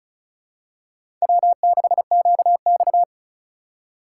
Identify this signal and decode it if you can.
Morse code, heard as a keyed tone, about 34 wpm, 700 Hz. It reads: W6QX
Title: USA Call Signs - 4 Characters - 4x Character Spacing - 35wpm